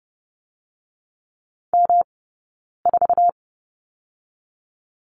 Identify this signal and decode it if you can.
Morse code, heard as a keyed tone, about 30 wpm, 700 Hz. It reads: M 4
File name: Lesson 18d - Letter m - Two Characters 30wpm